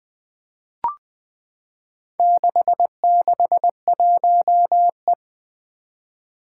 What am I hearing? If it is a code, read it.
Morse code, heard as a keyed tone, about 20 wpm, 700 Hz. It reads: 661E
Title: Four Challenging Letter-Number Combinations 20wpm